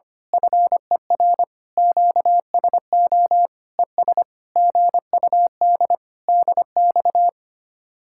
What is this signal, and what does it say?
Morse code, heard as a keyed tone, about 25 wpm, 700 Hz. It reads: FER QSO ES GUD DX
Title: Speed-Racing Common QSO Phrases - 25wpm